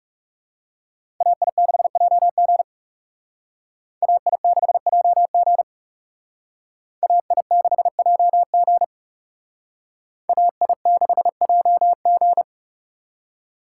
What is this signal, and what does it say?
Morse code, heard as a keyed tone, about 36 wpm, 700 Hz. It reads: AI6JG AI6JG AI6JG AI6JG
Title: Speed Racing - USA 2x2 Call Signs 30wpm